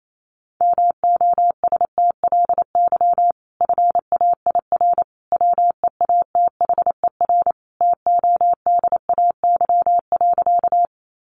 Morse code, heard as a keyed tone, about 28 wpm, 700 Hz. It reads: MOSTLY FAIR WEATHER TODAY.
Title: Sentences from Top 1300 Words 28wpm